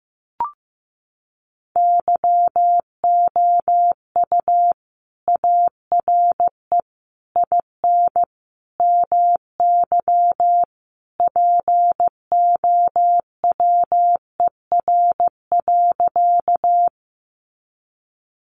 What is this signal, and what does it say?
Morse code, heard as a keyed tone, about 15 wpm, 700 Hz. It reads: YOU ARE IN MY POWER.
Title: Sentences from Top 500 Words 15wpm